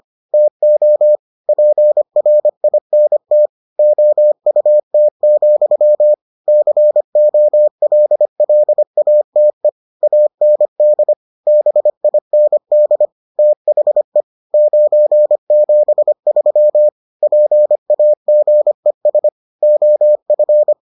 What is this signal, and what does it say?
Morse code, heard as a keyed tone, about 25 wpm, 600 Hz. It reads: TO PRINT OUT, COLLATE AND BIND THE 973 PAGES OF